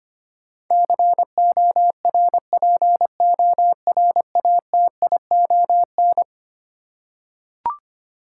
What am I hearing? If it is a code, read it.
Morse code, heard as a keyed tone, about 25 wpm, 700 Hz. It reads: CORPORATION
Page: morse-code-ninja-practice-25wpm